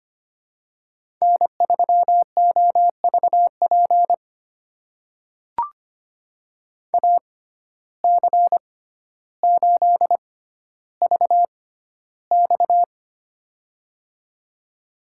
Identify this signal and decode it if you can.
Morse code, heard as a keyed tone, about 25 wpm, spaced as for 13 wpm, 700 Hz. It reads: N3OVP AC8VX
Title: USA Call Signs - 5 Characters - 6x Character Spacing - 25wpm